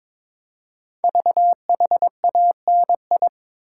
Morse code, heard as a keyed tone, about 22 wpm, 700 Hz. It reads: VHANI
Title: Code Groups - 5 Letters - 22wpm